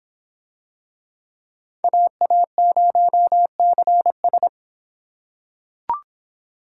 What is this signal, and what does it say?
Morse code, heard as a keyed tone, about 23 wpm, 700 Hz. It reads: AA0CS